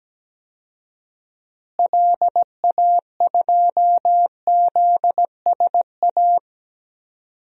Morse code, heard as a keyed tone, about 17 wpm, 700 Hz. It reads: LA2ZSA
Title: DXCC-100 Call Signs - 6 Characters - Repeated 2x - 17wpm